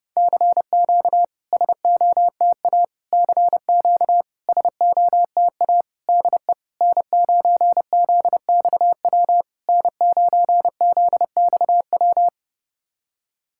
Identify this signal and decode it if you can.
Morse code, heard as a keyed tone, about 30 wpm, 690 Hz. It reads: CQ SOTA CQ SOTA DE N9ZXW N9ZXW
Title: SOTA QSO Practice 30wpm